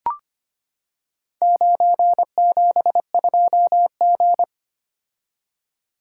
Morse code, heard as a keyed tone, about 25 wpm, 700 Hz. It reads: 972G
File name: practice / 25wpm / Four Letter-Number Combinations - 25wpm